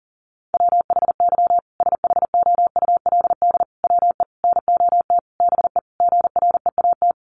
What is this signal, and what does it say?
Morse code, heard as a keyed tone, about 40 wpm, 700 Hz. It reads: WHY SHOULD WE NOT BE GREAT
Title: Sentences from Top 1500 Words 40wpm